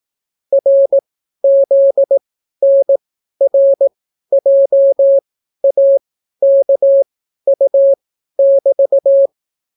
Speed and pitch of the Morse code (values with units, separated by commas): 18 wpm, spaced as for 12 wpm, 550 Hz